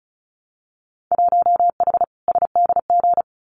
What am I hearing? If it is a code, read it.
Morse code, heard as a keyed tone, about 36 wpm, 700 Hz. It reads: 1H SDG
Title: Winter Field Day Exchanges 35wpm